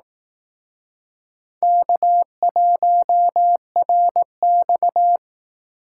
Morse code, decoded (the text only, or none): K1RX